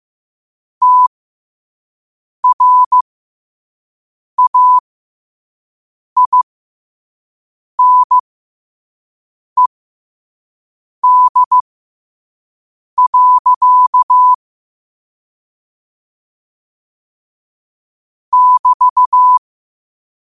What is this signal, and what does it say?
Morse code, heard as a keyed tone, about 15 wpm, spaced as for 5 wpm, 1000 Hz. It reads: TRAINED. <BT>
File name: qotdfiles / 5wpm